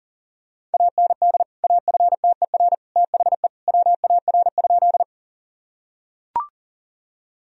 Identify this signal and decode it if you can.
Morse code, heard as a keyed tone, about 40 wpm, 700 Hz. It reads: AND AFTER THE WAR?
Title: Sentences from Top 1000 Words - Review Part 2 - 40wpm